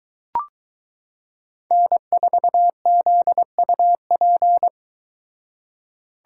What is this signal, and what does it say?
Morse code, heard as a keyed tone, about 23 wpm, 700 Hz. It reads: N4ZUP